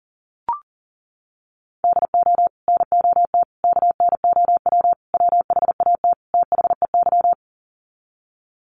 Morse code, heard as a keyed tone, about 37 wpm, 700 Hz. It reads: DO NOT KNOW WHAT THEY
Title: Sets of 5 Words 40wpm